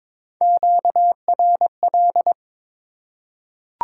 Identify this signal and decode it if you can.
Morse code, heard as a keyed tone, about 21 wpm, 700 Hz. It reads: QRL
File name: Top 90 QSO Elements - 4x Character Spacing - 22wpm